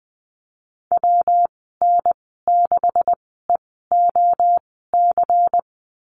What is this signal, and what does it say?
Morse code, heard as a keyed tone, about 20 wpm, spaced as for 14 wpm, 700 Hz. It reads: WN6EOC
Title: 147 Lesson 24g - Letter y - Call Signs 20wpm